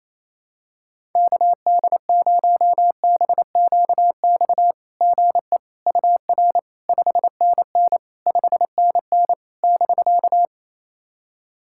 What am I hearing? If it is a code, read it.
Morse code, heard as a keyed tone, about 28 wpm, 690 Hz. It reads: KD0BQX GE UR 5NN 5NN <BK>
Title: SOTA QSO Practice 28wpm